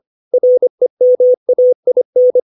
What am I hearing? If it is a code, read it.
Morse code, heard as a keyed tone, about 25 wpm, 500 Hz. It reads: REMAIN
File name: 632 words / 632 Words 25WPM